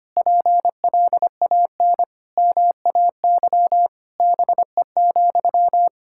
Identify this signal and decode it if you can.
Morse code, heard as a keyed tone, about 25 wpm, 700 Hz. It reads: PLAN MAY BE,
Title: Sentences from Top 1600 Words 25wpm